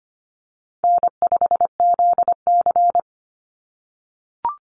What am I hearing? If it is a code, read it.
Morse code, heard as a keyed tone, about 25 wpm, 700 Hz. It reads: N5ZC
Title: NA SOTA Chaser Callsigns 25wpm